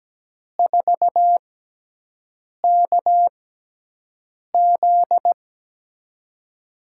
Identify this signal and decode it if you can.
Morse code, heard as a keyed tone, about 17 wpm, spaced as for 6 wpm, 700 Hz. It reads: 4KZ
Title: USA Call Signs - 5 Characters - 6x Character Spacing - 17wpm